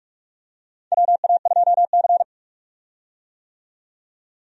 Morse code, heard as a keyed tone, about 46 wpm, 700 Hz. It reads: WA2C
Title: USA Call Signs - 4 Characters - 45wpm